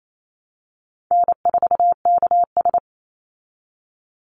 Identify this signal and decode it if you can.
Morse code, heard as a keyed tone, about 28 wpm, 700 Hz. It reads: N4KS
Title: Easy Call Signs Repeated 2x - 28wpm